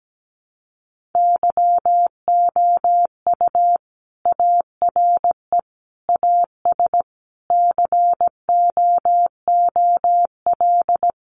Morse code, heard as a keyed tone, about 17 wpm, 700 Hz. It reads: YOU ARE AS COOL
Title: Sentences from Top 1500 Words 17wpm